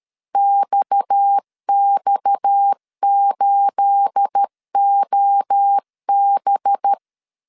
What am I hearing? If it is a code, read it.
Morse code, heard as a keyed tone, about 13 wpm, 800 Hz. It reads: XX8OB
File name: CW / DET / 1 9 1500C S A 12wpm 3800